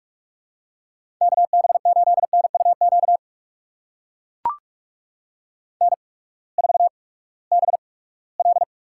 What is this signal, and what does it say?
Morse code, heard as a keyed tone, about 44 wpm, spaced as for 22 wpm, 700 Hz. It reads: KB8NUQ N4BL
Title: USA Call Signs - 8x Character Spacing - 45wpm